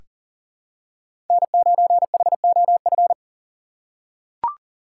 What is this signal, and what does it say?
Morse code, heard as a keyed tone, about 40 wpm, 700 Hz. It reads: N9HOF
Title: USA Call Signs - 5 Characters - 3x Character Spacing - 40wpm